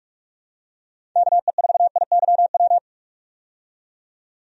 Morse code, heard as a keyed tone, about 46 wpm, 700 Hz. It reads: KE4IYW
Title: USA Call Signs - 6 Characters - 45wpm